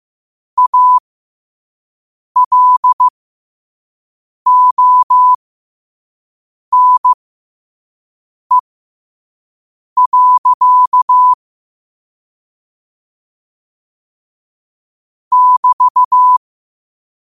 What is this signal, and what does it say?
Morse code, heard as a keyed tone, about 15 wpm, spaced as for 5 wpm, 1000 Hz. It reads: ALONE. <BT>